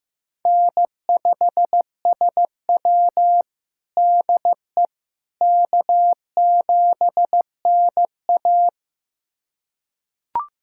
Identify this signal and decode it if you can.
Morse code, heard as a keyed tone, about 15 wpm, 700 Hz. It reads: N5SW DE K7NA
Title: USA Callsign DE Callsign - 4 Characters - 15wpm